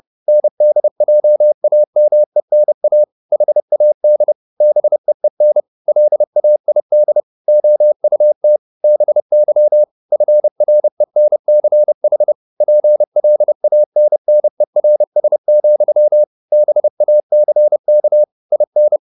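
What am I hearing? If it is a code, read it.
Morse code, heard as a keyed tone, about 30 wpm, 600 Hz. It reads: NDJAMENA HAD BEEN LAID OUT BY FRENCH PLANNERS, BACK IN
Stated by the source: Morse code at 30 wpm, the novel Never by Ken Follett in English, from CW med Gnister.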